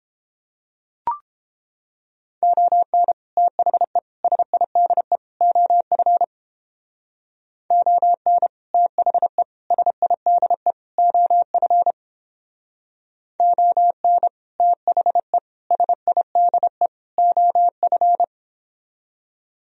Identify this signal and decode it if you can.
Morse code, heard as a keyed tone, about 29 wpm, 700 Hz. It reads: ON THE SIDE OF ON THE SIDE OF ON THE SIDE OF